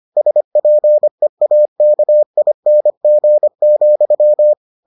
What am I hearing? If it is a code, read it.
Morse code, heard as a keyed tone, about 25 wpm, 600 Hz. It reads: SPEAKING,